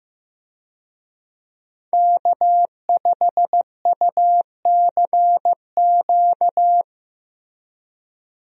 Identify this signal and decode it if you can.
Morse code, heard as a keyed tone, about 15 wpm, 700 Hz. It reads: K5UCQ
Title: USA Call Signs - 5 Characters - Repeated 3x - 15wpm